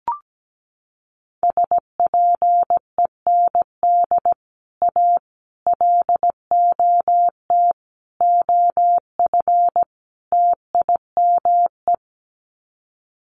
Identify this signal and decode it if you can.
Morse code, heard as a keyed tone, about 17 wpm, 700 Hz. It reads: SPEND A LOT OF TIME
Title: Sets of 5 Words 17wpm